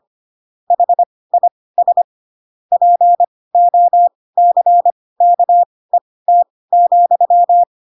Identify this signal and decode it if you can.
Morse code, heard as a keyed tone, about 25 wpm, spaced as for 18 wpm, 700 Hz. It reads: HIS POCKET,
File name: Pinocchio / Pinocchio ch7 12 25 ik1hge